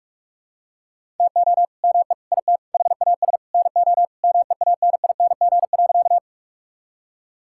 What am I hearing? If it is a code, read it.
Morse code, heard as a keyed tone, about 43 wpm, 700 Hz. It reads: TO ME IT HAS NO MEANING.
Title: Sentences from Top 1000 Words - Review Part 6 - 45wpm